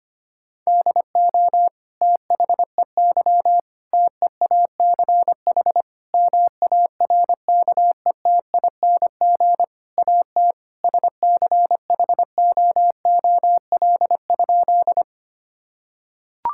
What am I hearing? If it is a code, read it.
Morse code, heard as a keyed tone, about 25 wpm, 700 Hz. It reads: DO THEY TEACH MARKETING AT SCHOOL?
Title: Sentences from Top 1000 Words - Review Part 5 - 25wpm